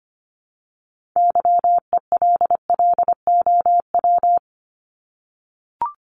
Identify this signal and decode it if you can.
Morse code, heard as a keyed tone, about 25 wpm, 700 Hz. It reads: YELLOW
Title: Top 1751-2000 Words 25wpm